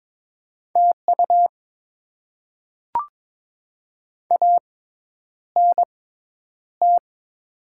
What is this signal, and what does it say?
Morse code, heard as a keyed tone, about 22 wpm, spaced as for 10 wpm, 690 Hz. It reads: TU ANT
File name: Top 20 QSO Elements - 6x Character Spacing - 22wpm